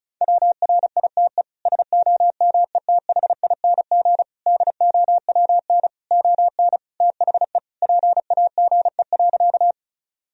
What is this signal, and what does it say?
Morse code, heard as a keyed tone, about 36 wpm, 700 Hz. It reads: WRITE SOMETHING DOWN ON THE PAGE.
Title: Sentences from Top 1000 Words - Review Part 1 - 35wpm